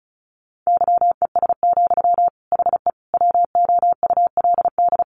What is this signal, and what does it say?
Morse code, heard as a keyed tone, about 34 wpm, 700 Hz. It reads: YES, HE WOULD
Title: Sentences from Top 900 Words 35wpm